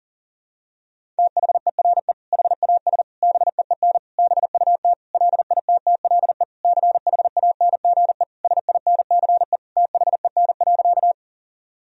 Morse code, heard as a keyed tone, about 40 wpm, 700 Hz. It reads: THERE HAS BEEN BUT LITTLE CHANGE SINCE THEN.
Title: Sentences from Top 1000 Words - Review Part 1 - 40wpm